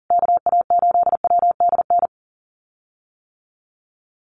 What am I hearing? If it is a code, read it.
Morse code, heard as a keyed tone, about 40 wpm, 700 Hz. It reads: KA8WDN
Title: USA Call Signs Repeated 3x 40wpm